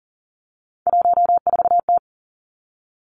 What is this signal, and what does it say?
Morse code, heard as a keyed tone, about 42 wpm, 690 Hz. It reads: 14T
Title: Three Challenging Letter-Number Combinations 40wpm